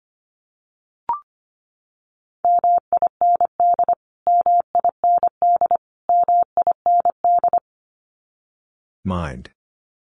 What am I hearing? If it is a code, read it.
Morse code, heard as a keyed tone, about 25 wpm, 700 Hz. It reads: MIND MIND MIND
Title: Top 500 Words Repeated 25wpm